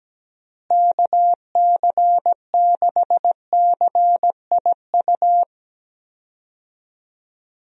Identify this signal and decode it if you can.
Morse code, heard as a keyed tone, about 17 wpm, 700 Hz. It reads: KC6CIU
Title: USA Call Signs - 6 Characters - Repeated 3x - 17wpm